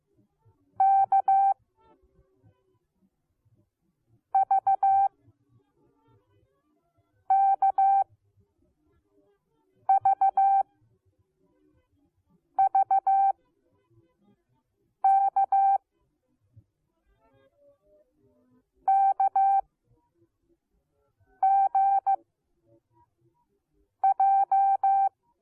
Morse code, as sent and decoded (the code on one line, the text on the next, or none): -.- / ...- / -.- / ...- / ...- / -.- / -.- / --. / .---
K V K V V K K G J